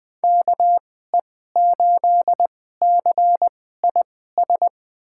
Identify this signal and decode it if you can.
Morse code, heard as a keyed tone, about 21 wpm, spaced as for 15 wpm, 700 Hz. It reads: KE8CIS
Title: USA Call Signs - 6 Characters - 2x Character Spacing - 20wpm